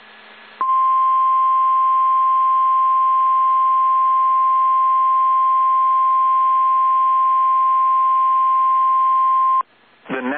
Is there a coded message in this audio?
Yes: Emergency Alert System (SAME) tones (the 1050 Hz attention tone) from 0.6 to 9.6 s.